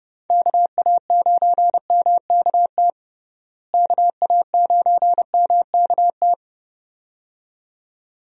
Morse code, heard as a keyed tone, about 32 wpm, 700 Hz. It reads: KA9MKT KA9MKT
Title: USA Call Signs - 6 Characters - Repeated 2x - 30wpm